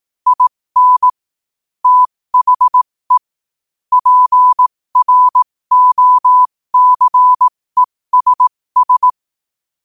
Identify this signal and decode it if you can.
Morse code, heard as a keyed tone, about 18 wpm, spaced as for 15 wpm, 1000 Hz. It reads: IN THE PROCESS
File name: qotdfiles / 15wpm